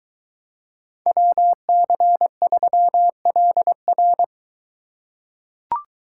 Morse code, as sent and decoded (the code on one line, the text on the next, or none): .-- -.-. ...-- .-.. .-.
WC3LR